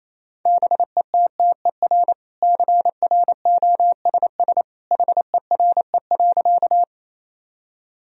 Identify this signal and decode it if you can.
Morse code, heard as a keyed tone, about 27 wpm, 700 Hz. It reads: BETTER CROSS HERE.
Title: Sentences from Top 1000 Words - Review Part 1 - 28wpm